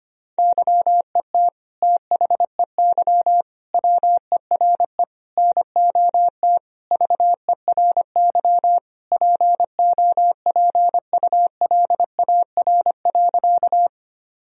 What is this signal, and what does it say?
Morse code, heard as a keyed tone, about 25 wpm, 700 Hz. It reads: YET THEY WERE NOT VERY POPULAR.
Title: Sentences from Top 1000 Words - Review Part 3 - 25wpm